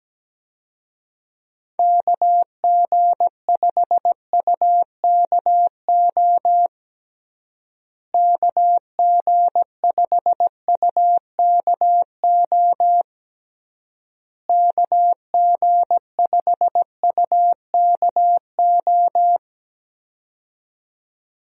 Morse code, heard as a keyed tone, about 17 wpm, 700 Hz. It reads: KG5UKO KG5UKO KG5UKO
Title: USA Call Signs - 6 Characters - Repeated 3x - 17wpm